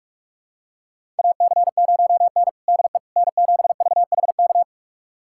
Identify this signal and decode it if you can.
Morse code, heard as a keyed tone, about 46 wpm, 700 Hz. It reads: AC0N DE N7VHK
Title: USA Callsign DE Callsign 45wpm